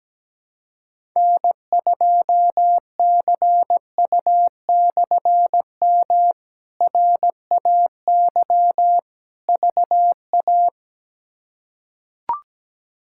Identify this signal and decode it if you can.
Morse code, heard as a keyed tone, about 17 wpm, 700 Hz. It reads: N2CU/M RAY VA
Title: K1USN SST QSO Practice - 17wpm